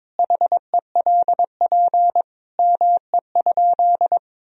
Morse code, heard as a keyed tone, about 22 wpm, 700 Hz. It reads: HELP ME?
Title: Sentences from Top 1000 Words - Review Part 3 - 22wpm